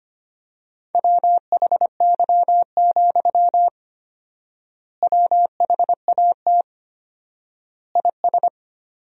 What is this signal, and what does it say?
Morse code, heard as a keyed tone, about 25 wpm, 700 Hz. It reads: WHY, WHAT IS
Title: 4x Word-Spacing - Sentences from Top 200 Words - 25wpm